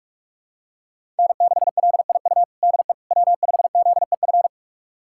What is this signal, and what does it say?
Morse code, heard as a keyed tone, about 46 wpm, 700 Hz. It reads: N6LIU DE W5ZEF
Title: USA Callsign DE Callsign - 5 Characters - 45wpm